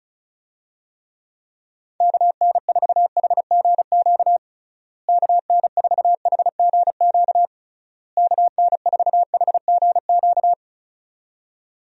Morse code, heard as a keyed tone, about 34 wpm, 700 Hz. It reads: KN4HGQ KN4HGQ KN4HGQ
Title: USA Call Signs - 6 Characters - Repeated 3x - 35wpm